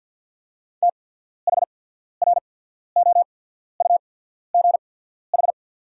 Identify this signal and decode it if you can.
Morse code, heard as a keyed tone, about 48 wpm, spaced as for 14 wpm, 690 Hz. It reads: THROUGH
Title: 8x Character Spacing - Top 200 Words - 50wpm